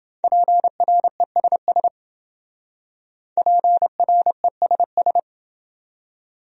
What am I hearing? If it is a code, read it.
Morse code, heard as a keyed tone, about 28 wpm, 700 Hz. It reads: PRESS PRESS